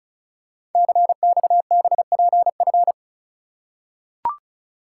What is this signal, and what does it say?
Morse code, heard as a keyed tone, about 36 wpm, 700 Hz. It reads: CXBPF
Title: Code Groups - 5 Letters - 35wpm